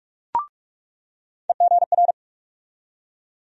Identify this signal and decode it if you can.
Morse code, heard as a keyed tone, about 46 wpm, 700 Hz. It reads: EGR
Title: USA Call Sign Postfixes - 3 Letters 45wpm